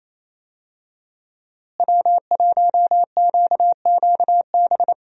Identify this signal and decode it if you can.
Morse code, heard as a keyed tone, about 28 wpm, 700 Hz. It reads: W1QQB
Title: USA Call Signs - 5 Characters - Repeated 2x - 28wpm